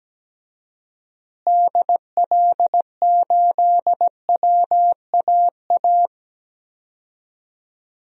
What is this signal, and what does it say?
Morse code, heard as a keyed tone, about 17 wpm, 700 Hz. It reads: DL8WAA
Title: DXCC-100 Call Signs - 6 Characters - Repeated 2x - 17wpm